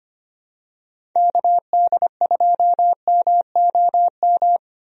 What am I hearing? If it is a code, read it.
Morse code, heard as a keyed tone, about 25 wpm, 700 Hz. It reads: KD2MOM
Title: USA Call Signs - 6 Characters - Repeated 3x - 25wpm